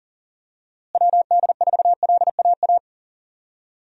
Morse code, heard as a keyed tone, about 40 wpm, 700 Hz. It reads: WD4LAA
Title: USA Call Signs - 6 Characters - Repeated 3x - 40wpm